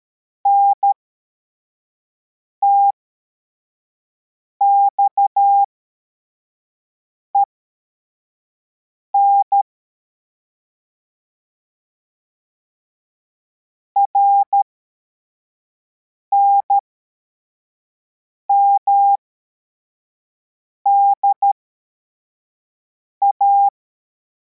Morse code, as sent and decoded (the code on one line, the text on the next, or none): -. - -..- . -. / .-. -. -- -.. .-
NTXEN RNMDA